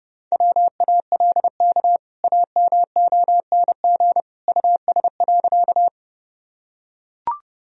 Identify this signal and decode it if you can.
Morse code, heard as a keyed tone, about 29 wpm, 700 Hz. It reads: WALK AMONG US.
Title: Sentences from Top 1000 Words - Review Part 3 - 30wpm